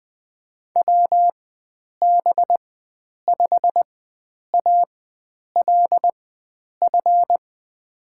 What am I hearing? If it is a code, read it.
Morse code, heard as a keyed tone, about 20 wpm, spaced as for 9 wpm, 700 Hz. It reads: WB5ALF
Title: USA Call Signs - 6 Characters - 4x Character Spacing - 20wpm